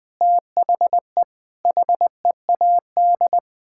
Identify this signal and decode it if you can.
Morse code, heard as a keyed tone, about 21 wpm, 700 Hz. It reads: THE HEAD
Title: Sets of 6 Words - 20wpm